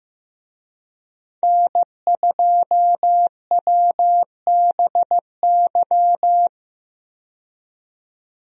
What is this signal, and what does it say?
Morse code, heard as a keyed tone, about 15 wpm, 700 Hz. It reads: N2WBY
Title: USA Call Signs - 5 Characters - 15wpm